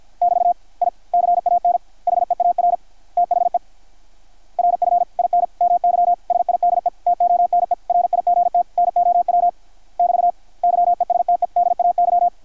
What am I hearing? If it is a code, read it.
Morse code, heard as a keyed tone, about 53 wpm, 720 Hz, with radio noise. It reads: <BT> I CAN HEAR THE RF IN MY SIDETONE RIGT NOW <BT> YESTEDAY
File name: archived / CRSnet